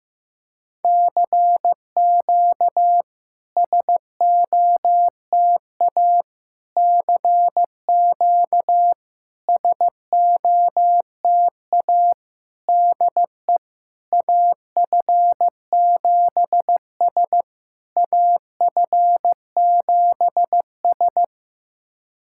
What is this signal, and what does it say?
Morse code, heard as a keyed tone, about 15 wpm, 700 Hz. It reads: CQ SOTA CQ SOTA DE AF7S AF7S
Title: SOTA QSO Practice 15wpm